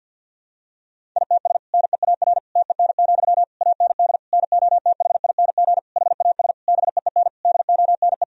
Morse code, heard as a keyed tone, about 52 wpm, 700 Hz. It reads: ITS NEAR TEN, AND NOTHING HAS BEEN DONE
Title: Sentences from Top 1000 Words - Review Part 2 - 50wpm